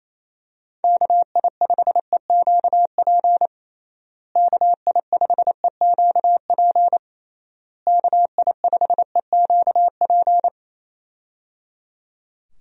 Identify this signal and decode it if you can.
Morse code, heard as a keyed tone, about 28 wpm, 700 Hz. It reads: KI5EQP KI5EQP KI5EQP
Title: USA Call Signs - 6 Characters - Repeated 3x - 28wpm